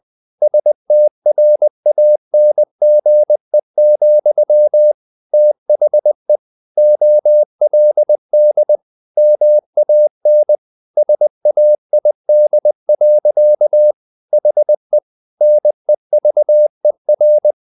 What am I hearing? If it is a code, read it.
Morse code, heard as a keyed tone, about 20 wpm, 600 Hz. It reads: STRANGE, THE OLD MAN SAID. HE NEVER
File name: the old man and the sea / the old man and the sea part 1 20wpm